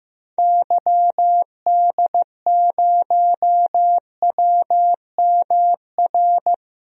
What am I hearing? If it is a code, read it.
Morse code, heard as a keyed tone, about 15 wpm, 700 Hz. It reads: YD0WMR
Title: DXCC-100 Call Signs - 6 Characters - Repeated 3x - 15wpm